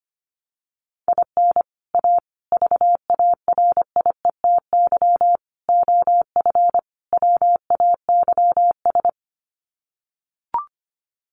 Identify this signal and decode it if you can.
Morse code, heard as a keyed tone, about 25 wpm, 700 Hz. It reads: IN A VARIETY OF WAYS
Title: Sets of 5 Words Encore 25wpm